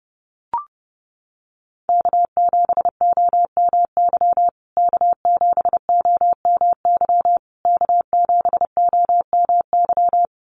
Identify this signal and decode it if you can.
Morse code, heard as a keyed tone, about 30 wpm, 700 Hz. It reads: K7OMY K7OMY K7OMY
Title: USA Call Signs Repeated 3x 30wpm